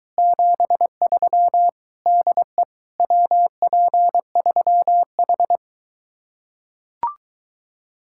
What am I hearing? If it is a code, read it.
Morse code, heard as a keyed tone, about 23 wpm, 700 Hz. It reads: 73 DE WP3H